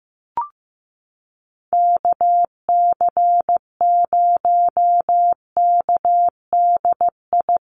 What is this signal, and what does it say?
Morse code, heard as a keyed tone, about 15 wpm, 700 Hz. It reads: KC0KDI
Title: USA Call Signs Repeated 3x 15wpm